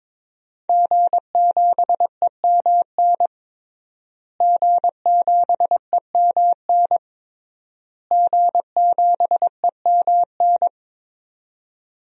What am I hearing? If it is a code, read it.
Morse code, heard as a keyed tone, about 22 wpm, 700 Hz. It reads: G7EMN G7EMN G7EMN
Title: DXCC-100 Call Signs - 5 Characters - Repeated 3x - 22wpm